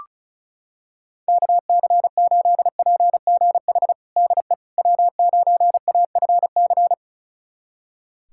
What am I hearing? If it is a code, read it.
Morse code, heard as a keyed tone, about 34 wpm, 700 Hz. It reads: KC8PGH DE W9AFC
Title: USA Callsign DE Callsign 35wpm